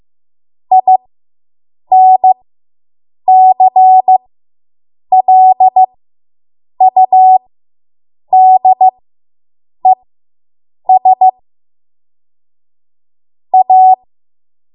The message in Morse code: INCLUDES A